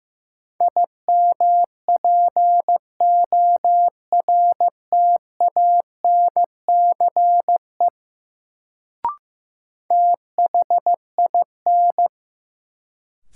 Morse code, heard as a keyed tone, about 15 wpm, 700 Hz. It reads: IMPORTANCE THIN
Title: Sentences from Top 1200 Words - New Word Drill Alt - 15wpm